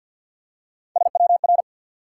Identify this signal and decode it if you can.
Morse code, heard as a keyed tone, about 47 wpm, 700 Hz. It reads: SWR
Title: Top 180 QSO Elements 50wpm